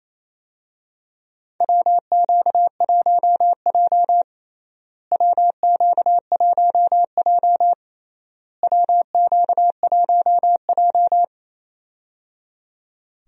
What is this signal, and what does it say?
Morse code, heard as a keyed tone, about 28 wpm, 700 Hz. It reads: WQ1J WQ1J WQ1J
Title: USA Call Signs - 4 Characters - Repeated 3x - 28wpm